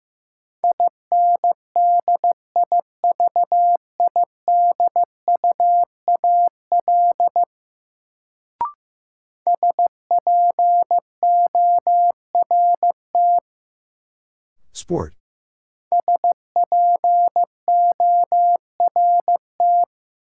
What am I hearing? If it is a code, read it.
Morse code, heard as a keyed tone, about 15 wpm, 700 Hz. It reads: INDIVIDUAL SPORT SPORT
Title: Sentences from Top 1000 Words - Review Part 4 - New Word Drill - 15wpm